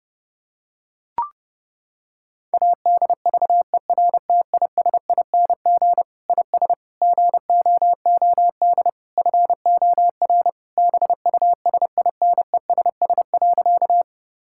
Morse code, heard as a keyed tone, about 30 wpm, 700 Hz. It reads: ADVERTISING IS GOOD FOR BUSINESS.
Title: Sentences from Top 1000 Words - Review Part 6 - 30wpm